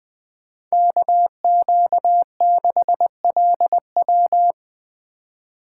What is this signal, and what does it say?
Morse code, heard as a keyed tone, about 20 wpm, 700 Hz. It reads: KQ6LW